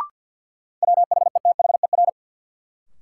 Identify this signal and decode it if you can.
Morse code, heard as a keyed tone, about 46 wpm, 700 Hz. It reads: WHETHER
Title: Sentences from Top 1300 Words - New Word Drill Alt - 50wpm